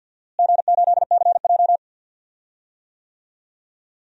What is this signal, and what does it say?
Morse code, heard as a keyed tone, about 48 wpm, 700 Hz. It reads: G8CJ
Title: DXCC-100 Call Signs - 4 Characters - Repeated 3x - 50wpm